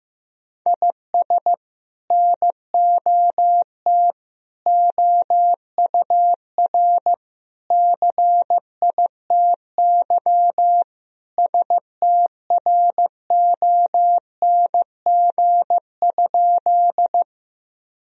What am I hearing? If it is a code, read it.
Morse code, heard as a keyed tone, about 15 wpm, 700 Hz. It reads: IS NOT OUR CITY STRONG?
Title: Sentences from Top 1000 Words - Review Part 3 - 15wpm